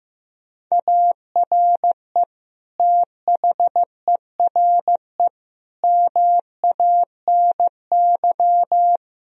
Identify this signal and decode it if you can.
Morse code, heard as a keyed tone, about 15 wpm, 700 Hz. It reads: ARE THERE MANY
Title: Sentences from Top 1000 Words - Review Part 1 - 15wpm